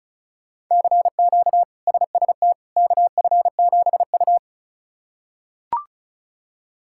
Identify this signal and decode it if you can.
Morse code, heard as a keyed tone, about 36 wpm, 700 Hz. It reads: CQ SST KF7U
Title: K1USN SST QSO Practice - 35wpm